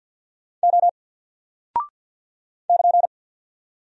Morse code, heard as a keyed tone, about 48 wpm, 700 Hz. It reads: O <KN>
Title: Single Character (Letter:Number:Punctuation:Prosign) Alt 50wpm